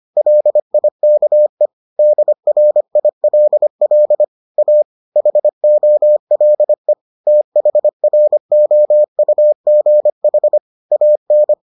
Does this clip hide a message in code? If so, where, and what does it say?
Yes: Morse code from 0.2 to 11.5 s, heard as a keyed tone, about 25 wpm, 600 Hz. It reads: LIKE DRILL A HOLE THROUGH AN